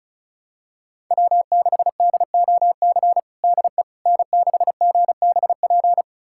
Morse code, heard as a keyed tone, about 36 wpm, 700 Hz. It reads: W6DOC DE N6GBP
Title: USA Callsign DE Callsign - 5 Characters - 35wpm